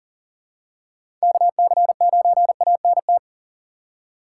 Morse code, heard as a keyed tone, about 40 wpm, 700 Hz. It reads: KC9ANT
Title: USA Call Signs - 4x Character Spacing - 40wpm